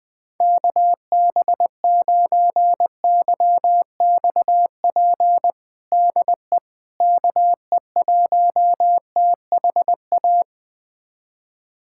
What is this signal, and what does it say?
Morse code, heard as a keyed tone, about 20 wpm, 700 Hz. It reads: KB9YXP DE KE1THA
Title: USA Callsign DE Callsign 20wpm